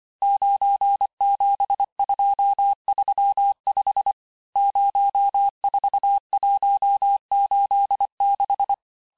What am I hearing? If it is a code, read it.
Morse code, heard as a keyed tone, about 25 wpm, 800 Hz. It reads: 97235 04186
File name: texte-chiffre-vitesse-23mpm